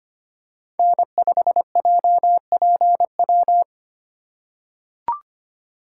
Morse code, heard as a keyed tone, about 25 wpm, 700 Hz. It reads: N5JPW
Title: USA Call Signs - 8x Character Spacing - 25wpm